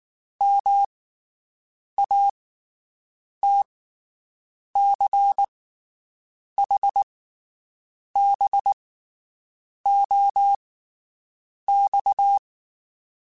Morse code, heard as a keyed tone, about 19 wpm, spaced as for 7 wpm, 800 Hz. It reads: MATCHBOX
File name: QSOs Spaced